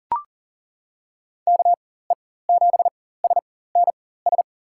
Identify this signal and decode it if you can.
Morse code, heard as a keyed tone, about 40 wpm, spaced as for 19 wpm, 700 Hz. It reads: KE7SNS
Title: USA Call Signs - 6 Characters - 4x Character Spacing - 40wpm